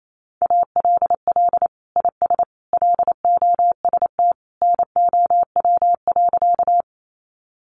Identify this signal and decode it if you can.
Morse code, heard as a keyed tone, about 28 wpm, 700 Hz. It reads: ALL IS LOST NOW.
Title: Sentences from Top 1400 Words 28wpm